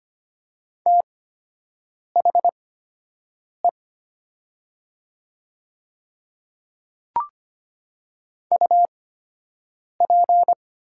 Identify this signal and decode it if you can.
Morse code, heard as a keyed tone, about 25 wpm, spaced as for 7 wpm, 700 Hz. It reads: THE UP
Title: 8x Character Spacing - Top 100 Words - 25wpm